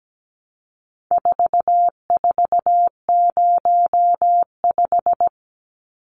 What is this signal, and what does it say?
Morse code, heard as a keyed tone, about 17 wpm, 700 Hz. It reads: 4405
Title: Numbers - 4-Digit 17wpm